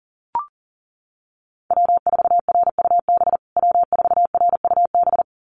Morse code, heard as a keyed tone, about 40 wpm, 700 Hz. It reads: W4RUB W4RUB
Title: USA Call Signs Repeated 2x 40wpm